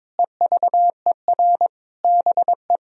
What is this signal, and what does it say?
Morse code, heard as a keyed tone, about 21 wpm, 700 Hz. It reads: EVER BE
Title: Sentences from Top 1000 Words - Review Part 4 - 22wpm